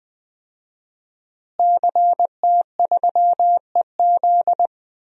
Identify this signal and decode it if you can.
Morse code, heard as a keyed tone, about 20 wpm, 700 Hz. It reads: CT3EZ
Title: DXCC-100 Call Signs - 5 Characters - Repeated 3x - 20wpm